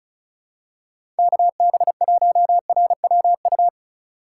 Morse code, heard as a keyed tone, about 36 wpm, 700 Hz. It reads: KB1RWU
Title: USA Call Signs - 6 Characters - 4x Character Spacing - 35wpm